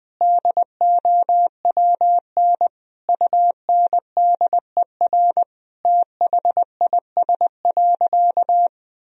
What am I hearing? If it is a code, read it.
Morse code, heard as a keyed tone, about 19 wpm, 700 Hz. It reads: DOWN UNDER THIS.
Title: Sentences from Top 1000 Words - Review Part 1 - 20wpm